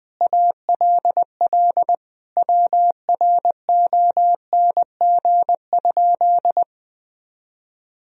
Morse code, heard as a keyed tone, about 20 wpm, 700 Hz. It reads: ALL WRONG?
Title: Sentences from Top 1000 Words - Review Part 1 - 20wpm